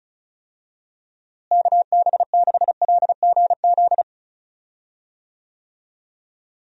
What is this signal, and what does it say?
Morse code, heard as a keyed tone, about 34 wpm, 700 Hz. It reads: KB6LGZ
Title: USA Call Signs - 6 Characters - Repeated 2x - 35wpm